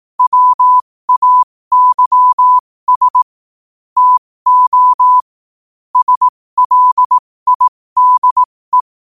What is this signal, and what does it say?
Morse code, heard as a keyed tone, about 18 wpm, spaced as for 15 wpm, 1000 Hz. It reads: WAYS TO SLIDE
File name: qotdfiles / 15wpm